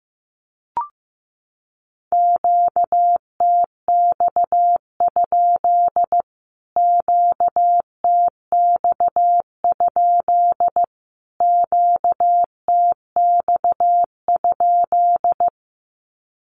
Morse code, heard as a keyed tone, about 15 wpm, 700 Hz. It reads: QTX? QTX? QTX?
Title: Q-Codes Repeated 3x 15wpm